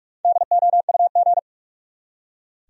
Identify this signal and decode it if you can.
Morse code, heard as a keyed tone, about 46 wpm, 700 Hz. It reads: DOUG
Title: USA Common Names - Max 5 Letters - 45wpm